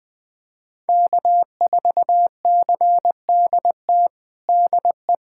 Morse code, heard as a keyed tone, about 20 wpm, 700 Hz. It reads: K4CDT DE
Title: USA Callsign DE Callsign - 5 Characters - 20wpm